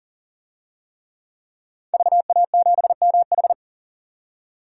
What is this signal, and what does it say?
Morse code, heard as a keyed tone, about 40 wpm, 700 Hz. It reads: VA7MH
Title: DXCC-100 Call Signs - 5 Characters - 40wpm